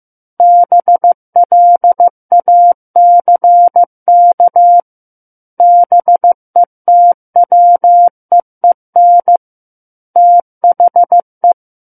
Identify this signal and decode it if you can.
Morse code, heard as a keyed tone, about 15 wpm, 700 Hz. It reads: BLACK BETWEEN THE
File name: Book One / 06-Chapter 6 - 15 WPM